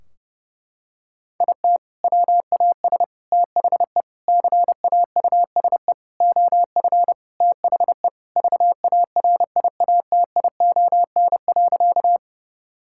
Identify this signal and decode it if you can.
Morse code, heard as a keyed tone, about 30 wpm, 700 Hz. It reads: IT WAS THE CAUSE OF THE VARIATION.
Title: Sentences from Top 1600 Words 30wpm